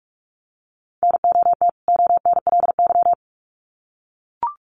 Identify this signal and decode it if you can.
Morse code, heard as a keyed tone, about 44 wpm, 700 Hz. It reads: NOT ONLY
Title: Sets of 2 Words 45wpm